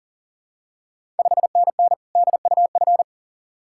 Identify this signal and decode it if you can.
Morse code, heard as a keyed tone, about 40 wpm, 700 Hz. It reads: 5NN DUF
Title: Ontario QSO Party Exchange 40wpm